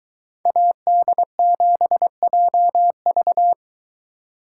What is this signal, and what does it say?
Morse code, heard as a keyed tone, about 21 wpm, 700 Hz. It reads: AD7JV